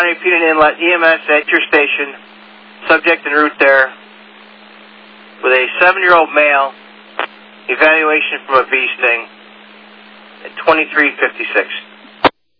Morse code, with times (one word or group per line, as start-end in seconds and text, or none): none